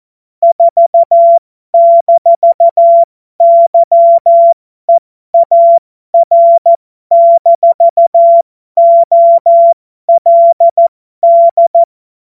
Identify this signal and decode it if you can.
Morse code, heard as a keyed tone, about 14 wpm, spaced as for 12 wpm, 680 Hz. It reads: 4-YEAR-OLD
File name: Morsekurs / Witze